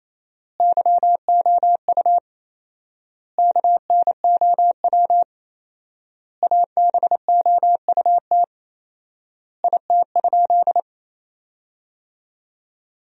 Morse code, heard as a keyed tone, about 28 wpm, 700 Hz. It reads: YOU KNOW ABOUT IT?
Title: 4x Word-Spacing - Sentences from Top 100 Words - 28wpm